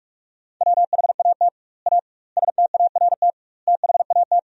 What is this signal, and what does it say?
Morse code, heard as a keyed tone, about 46 wpm, 700 Hz. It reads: WHAT A START THAT
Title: 2x Word-Spacing - Sentences from Top 300 Words - 45wpm